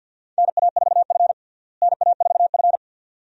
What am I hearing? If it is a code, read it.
Morse code, heard as a keyed tone, about 52 wpm, 700 Hz. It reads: NA4F NA4F
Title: USA Call Signs - 4 Characters - Repeated 2x - 50wpm